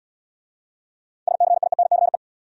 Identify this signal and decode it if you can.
Morse code, heard as a keyed tone, about 65 wpm, 700 Hz. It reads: SCIENCE